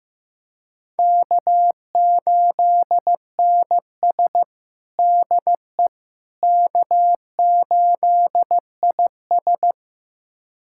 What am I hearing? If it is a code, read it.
Morse code, heard as a keyed tone, about 15 wpm, 700 Hz. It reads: K8NS DE K8IS
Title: USA Callsign DE Callsign - 4 Characters - 15wpm